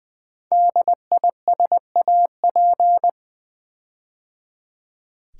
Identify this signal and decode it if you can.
Morse code, heard as a keyed tone, about 20 wpm, 700 Hz. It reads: DISAP
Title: Build A Word - 5 Letters At A Time - 12-Letter Words - 20wpm